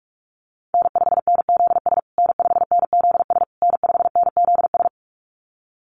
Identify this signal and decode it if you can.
Morse code, heard as a keyed tone, about 44 wpm, 700 Hz. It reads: N5NZS N5NZS N5NZS
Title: USA Call Signs Repeated 3x 45wpm